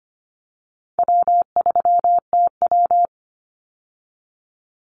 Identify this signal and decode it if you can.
Morse code, heard as a keyed tone, about 25 wpm, 700 Hz. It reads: W3TW
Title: Call Signs - 25wpm